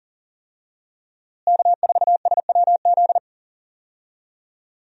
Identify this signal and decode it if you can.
Morse code, heard as a keyed tone, about 40 wpm, 700 Hz. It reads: K4SWZ
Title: USA Call Signs - 5 Characters - Repeated 2x - 40wpm